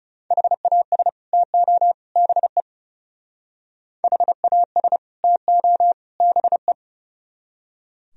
Morse code, heard as a keyed tone, about 39 wpm, 700 Hz. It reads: HAS TO BE HAS TO BE